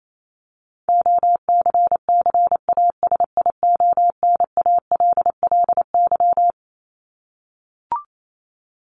Morse code, heard as a keyed tone, about 28 wpm, 700 Hz. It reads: OCCASIONALLY
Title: Top 1001-1250 Words 28wpm